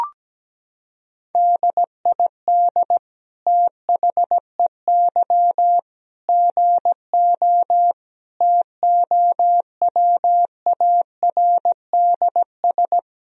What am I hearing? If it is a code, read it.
Morse code, heard as a keyed tone, about 17 wpm, 700 Hz. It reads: DID THEY GO TOWARDS
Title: Sentences from Top 1000 Words - Review Part 2 - 17wpm